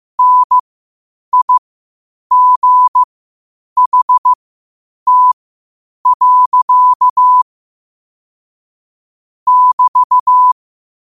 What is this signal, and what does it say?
Morse code, heard as a keyed tone, about 15 wpm, spaced as for 8 wpm, 1000 Hz. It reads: NIGHT. <BT>
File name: qotdfiles / 8wpm